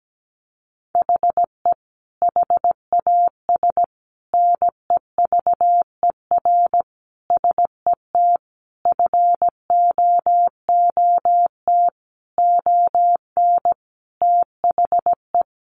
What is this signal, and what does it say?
Morse code, heard as a keyed tone, about 17 wpm, 700 Hz. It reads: HE HAS NEVER SET FOOT ON THE
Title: Sentences from Top 1500 Words 17wpm